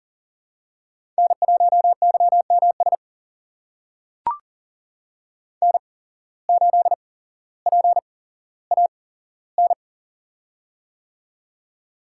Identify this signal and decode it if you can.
Morse code, heard as a keyed tone, about 41 wpm, spaced as for 17 wpm, 700 Hz. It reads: N1YMS N8PAN
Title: USA Call Signs - 5 Characters - 8x Character Spacing - 40wpm